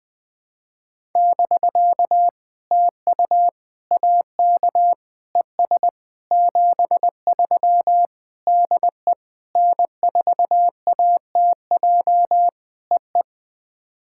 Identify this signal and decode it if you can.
Morse code, heard as a keyed tone, about 20 wpm, 700 Hz. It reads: <BK> TU AK ES 73 DE N4ATJ EE
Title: POTA QSO Practice - 20wpm